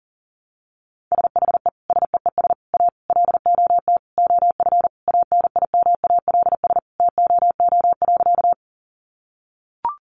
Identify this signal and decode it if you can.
Morse code, heard as a keyed tone, about 40 wpm, 700 Hz. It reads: SHE SEES A LOT OF ANIMALS TOO.
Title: Easy Phrases 40wpm